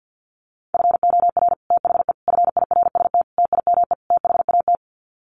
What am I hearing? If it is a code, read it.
Morse code, heard as a keyed tone, about 50 wpm, 700 Hz. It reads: FOR THE FIRST TIME THAT
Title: Sets of 5 Words Encore 50wpm